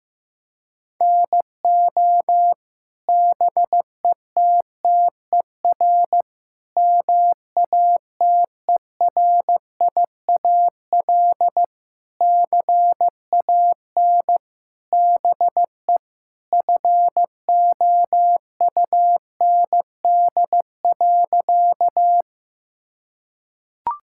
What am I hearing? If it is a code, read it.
Morse code, heard as a keyed tone, about 15 wpm, 700 Hz. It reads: NO BETTER MATERIAL CAN BE FOUND.
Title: Sentences from Top 1000 Words - Review Part 5 - 15wpm